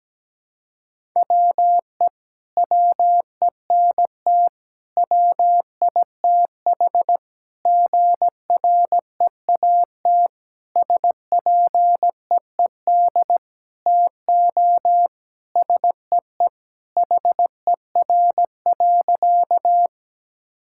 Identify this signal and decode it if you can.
Morse code, heard as a keyed tone, about 17 wpm, 700 Hz. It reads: WE WENT WITH GREAT SPEED TO SEE HER.
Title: Sentences from Top 1000 Words - Review Part 5 - 17wpm